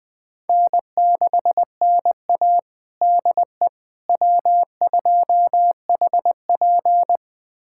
Morse code, heard as a keyed tone, about 21 wpm, 700 Hz. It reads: N6NA DE W2HP
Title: USA Callsign DE Callsign - 4 Characters - 20wpm